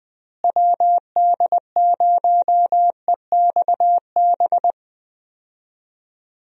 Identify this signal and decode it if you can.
Morse code, heard as a keyed tone, about 20 wpm, 700 Hz. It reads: WD0EXB
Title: USA Call Signs - 6 Characters - Repeated 2x - 20wpm